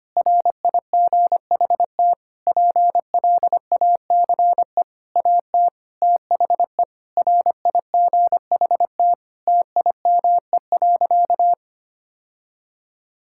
Morse code, heard as a keyed tone, about 25 wpm, 700 Hz. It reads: RIGHT PLACE AT THE RIGHT TIME.
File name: practice / 25wpm / Sentences with 9-Word Phrases - 25wpm